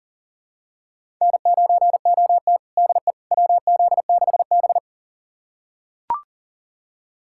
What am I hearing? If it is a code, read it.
Morse code, heard as a keyed tone, about 40 wpm, 700 Hz. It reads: N9OT DE WZ6B
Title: USA Callsign DE Callsign - 4 Characters - 40wpm